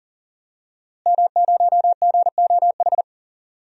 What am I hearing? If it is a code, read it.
Morse code, heard as a keyed tone, about 40 wpm, 700 Hz. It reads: M0GOH
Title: DXCC-100 Call Signs - 5 Characters - Repeated 2x - 40wpm